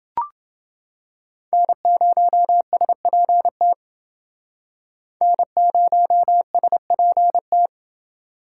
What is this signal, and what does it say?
Morse code, heard as a keyed tone, about 30 wpm, 700 Hz. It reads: N0SPT N0SPT